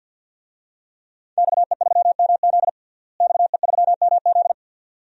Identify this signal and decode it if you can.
Morse code, heard as a keyed tone, about 52 wpm, 700 Hz. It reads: XE3MZ XE3MZ
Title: DXCC-100 Call Signs - 5 Characters - Repeated 2x - 50wpm